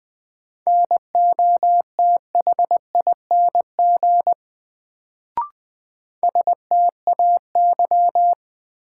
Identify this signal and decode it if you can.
Morse code, heard as a keyed tone, about 20 wpm, 700 Hz. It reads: NOTHING STAY
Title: Sentences from Top 500 Words - New Word Drill - 20wpm